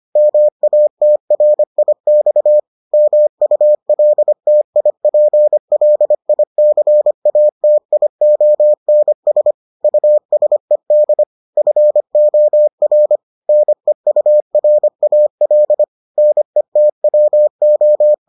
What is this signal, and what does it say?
Morse code, heard as a keyed tone, about 25 wpm, 600 Hz. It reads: MATRIX MULTIPLICATIONS USED FOR NEURAL NETWO